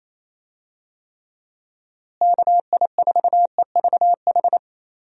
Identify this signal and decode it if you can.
Morse code, heard as a keyed tone, about 27 wpm, 700 Hz. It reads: KI4EVH
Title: USA Call Signs - 6 Characters - Repeated 3x - 28wpm